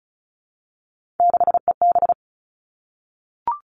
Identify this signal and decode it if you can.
Morse code, heard as a keyed tone, about 36 wpm, 700 Hz. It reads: 6EB
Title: Three Challenging Letter-Number Combinations 35wpm